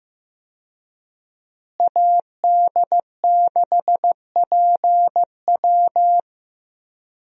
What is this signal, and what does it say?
Morse code, heard as a keyed tone, about 15 wpm, 700 Hz. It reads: AD6PW
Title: USA Call Signs - 5 Characters - Repeated 2x - 15wpm